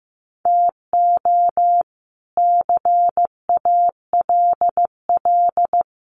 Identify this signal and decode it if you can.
Morse code, heard as a keyed tone, about 15 wpm, 700 Hz. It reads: TO CALL
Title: Sentences from Top 1400 Words 15wpm